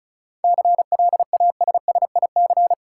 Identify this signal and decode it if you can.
Morse code, heard as a keyed tone, about 34 wpm, 700 Hz. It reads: CLASSIC
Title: Sentences from Top 1000 Words - New Word Drill - 35wpm